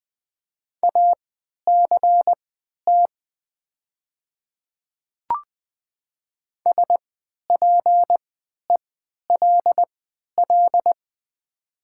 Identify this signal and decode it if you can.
Morse code, heard as a keyed tone, about 20 wpm, spaced as for 11 wpm, 700 Hz. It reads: ACT SPELL
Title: 3x Character Spacing - Top 200 Words - 20wpm